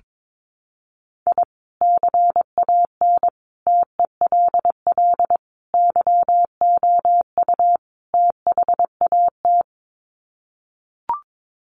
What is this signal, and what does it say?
Morse code, heard as a keyed tone, about 22 wpm, 700 Hz. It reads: I CAN TELL YOU THAT
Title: Sets of 5 Words 22wpm